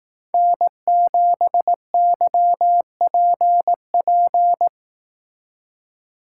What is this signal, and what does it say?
Morse code, heard as a keyed tone, about 18 wpm, 700 Hz. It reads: N7YPP
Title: Speed Racing - USA 1x3 Call Signs 15wpm